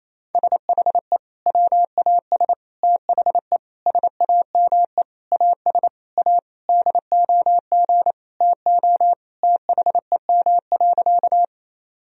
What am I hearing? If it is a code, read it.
Morse code, heard as a keyed tone, about 27 wpm, 700 Hz. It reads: SHE WAS THE SAME AS A DOG TO THEM.
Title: Sentences from Top 1000 Words - Review Part 2 - 28wpm